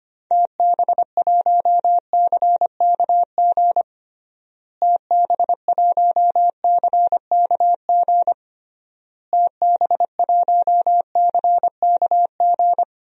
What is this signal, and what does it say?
Morse code, heard as a keyed tone, about 25 wpm, 700 Hz. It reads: TB1CKG TB1CKG TB1CKG
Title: DXCC-100 Call Signs - 6 Characters - Repeated 3x - 25wpm